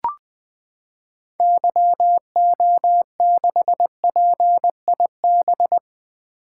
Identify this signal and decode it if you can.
Morse code, heard as a keyed tone, about 20 wpm, 700 Hz. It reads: YO6PIB
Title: 2023 SOTA Activator Call Signs - 20wpm